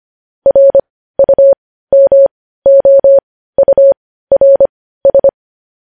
Morse code, heard as a keyed tone, about 25 wpm, spaced as for 15 wpm, 550 Hz. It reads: RUMOURS
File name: podcast / News headlines-1x-25x15